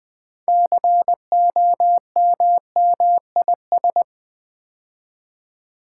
Morse code, heard as a keyed tone, about 20 wpm, 700 Hz. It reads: COMMIS
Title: Build A Word - 6 Letters At A Time - 12-Letter Words - 20wpm